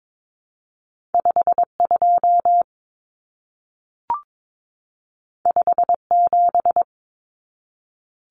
Morse code, heard as a keyed tone, about 22 wpm, 700 Hz. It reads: 52 57
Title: Common Cut Numbers - 2-Digit 22wpm